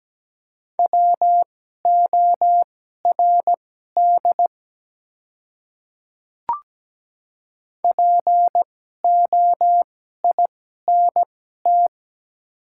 Morse code, heard as a keyed tone, about 17 wpm, spaced as for 12 wpm, 700 Hz. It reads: WORD POINT
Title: 2x Character Spacing - Top 200 Words - 17wpm